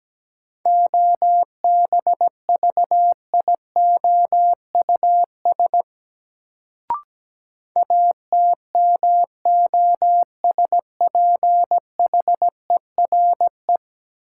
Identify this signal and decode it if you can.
Morse code, heard as a keyed tone, about 17 wpm, 700 Hz. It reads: OBVIOUS ATMOSPHERE
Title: Sentences from Top 1600 Words - New Word Drill Alt - 17wpm